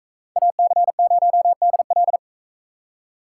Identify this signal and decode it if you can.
Morse code, heard as a keyed tone, about 38 wpm, 700 Hz. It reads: AC0DL